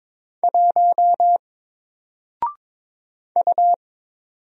Morse code, heard as a keyed tone, about 22 wpm, 700 Hz. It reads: 1 U
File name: Single-Letter-Number-22wpm